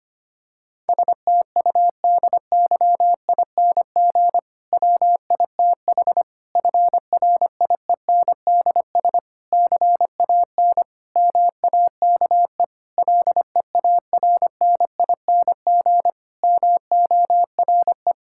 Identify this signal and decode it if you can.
Morse code, heard as a keyed tone, about 25 wpm, 700 Hz. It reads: STUDYING WITH FRIENDS CAN MAKE LEARNING MORE
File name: practice / 25wpm / Difficult Sentences with Hints - 25wpm